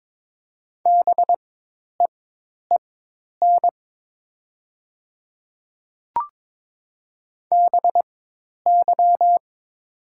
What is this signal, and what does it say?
Morse code, heard as a keyed tone, about 22 wpm, spaced as for 10 wpm, 700 Hz. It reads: BEEN BY
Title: 4x Character Spacing - Top 100 Words - 22wpm